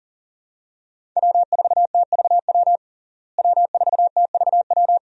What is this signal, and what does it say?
Morse code, heard as a keyed tone, about 40 wpm, 700 Hz. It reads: W4TVW W4TVW
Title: USA Call Signs - 5 Characters - Repeated 3x - 40wpm